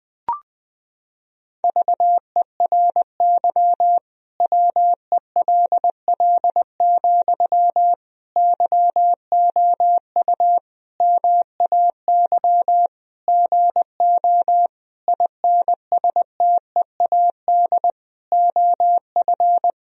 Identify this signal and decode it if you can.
Morse code, heard as a keyed tone, about 20 wpm, 700 Hz. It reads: VERY WELL, YOU MAY GO INSTEAD OF
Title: Sentences from Top 1000 Words - Review Part 4 - 20wpm